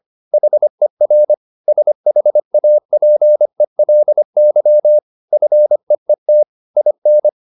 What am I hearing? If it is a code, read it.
Morse code, heard as a keyed tone, about 25 wpm, 600 Hz. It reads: HER SHAPELY FEET IN